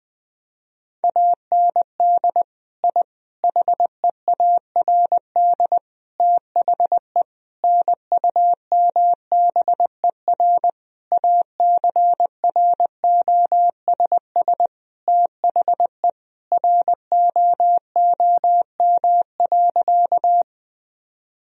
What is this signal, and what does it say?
Morse code, heard as a keyed tone, about 21 wpm, 700 Hz. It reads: AND I HEARD THE NUMBER ACROSS THE ROOM.
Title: Sentences from Top 1000 Words - Review Part 1 - 20wpm